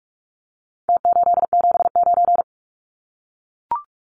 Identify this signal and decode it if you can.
Morse code, heard as a keyed tone, about 44 wpm, 700 Hz. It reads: T879
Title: Numbers - Contest Style - 4-Digit 45wpm